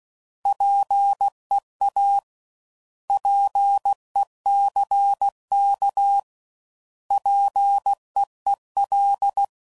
Morse code, heard as a keyed tone, about 16 wpm, 800 Hz. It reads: PEA PECK PEEL
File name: Prefixes / By First Letter / Ps